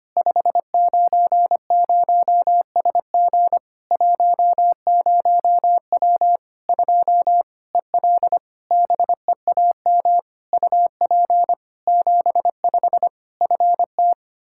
Morse code, heard as a keyed tone, about 25 wpm, 700 Hz. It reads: 590SG 10W 2 EL BEAM UP 75 FT
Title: Practice QSO - Exchange 2 - Info - 25wpm